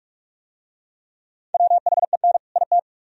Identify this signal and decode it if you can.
Morse code, heard as a keyed tone, about 42 wpm, 700 Hz. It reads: WHEN IT